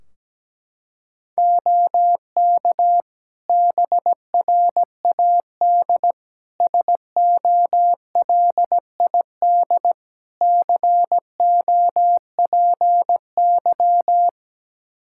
Morse code, heard as a keyed tone, about 17 wpm, 700 Hz. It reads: OK BRAD SOLID COPY
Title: Practice QSO - Exchange 3 - More Info - 17wpm